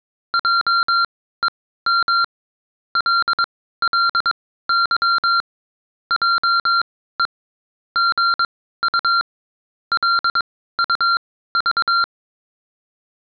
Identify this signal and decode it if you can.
Morse code, heard as a keyed tone, about 22 wpm, spaced as for 15 wpm, 1380 Hz. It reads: JEM LLY JE GU LUV